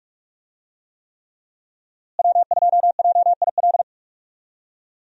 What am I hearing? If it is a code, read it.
Morse code, heard as a keyed tone, about 46 wpm, 700 Hz. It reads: W2JIL
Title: USA Call Signs - 5 Characters - Repeated 3x - 45wpm